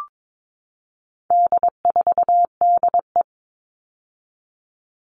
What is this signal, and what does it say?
Morse code, heard as a keyed tone, about 22 wpm, 700 Hz. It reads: D4DE
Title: All Cut Numbers - 4-Digit 22wpm